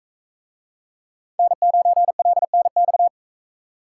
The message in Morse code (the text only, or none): N9LNX